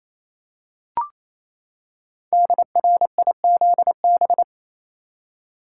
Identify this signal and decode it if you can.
Morse code, heard as a keyed tone, about 29 wpm, 700 Hz. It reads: DRIZB
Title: Code Groups - 5 Letters - 28wpm